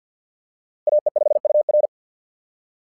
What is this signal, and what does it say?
Morse code, heard as a keyed tone, about 50 wpm, 600 Hz. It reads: AE5UR
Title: SOTA QSO Practice 50wpm